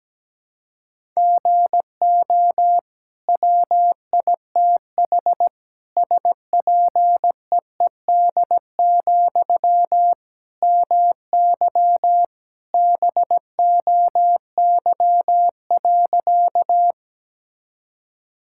Morse code, heard as a keyed tone, about 17 wpm, 700 Hz. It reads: GO WITH SPEED, MY BOY.
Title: Sentences from Top 1000 Words - Review Part 5 - 17wpm